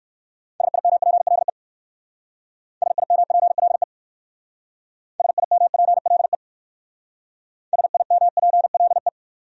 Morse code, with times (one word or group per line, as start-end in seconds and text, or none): 0.6-1.5 SIMPLE
2.8-3.8 SIMPLE
5.2-6.3 SIMPLE
7.7-9.1 SIMPLE